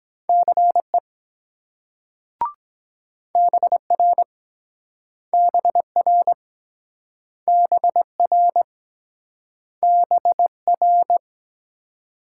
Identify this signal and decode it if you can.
Morse code, heard as a keyed tone, about 24 wpm, 700 Hz. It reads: CE BR BR BR BR